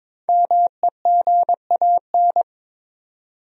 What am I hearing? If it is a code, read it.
Morse code, heard as a keyed tone, about 22 wpm, 700 Hz. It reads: MEGAN
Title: USA Common Names - Max 5 Letters - 22wpm